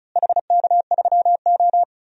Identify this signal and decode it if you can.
Morse code, heard as a keyed tone, about 36 wpm, 700 Hz. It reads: HK3O
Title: DXCC-100 Call Signs - 4 Characters - Repeated 2x - 35wpm